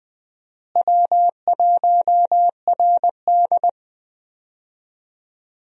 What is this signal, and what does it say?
Morse code, heard as a keyed tone, about 20 wpm, 700 Hz. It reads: W1RD
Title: USA Call Signs - 4 Characters - Repeated 2x - 20wpm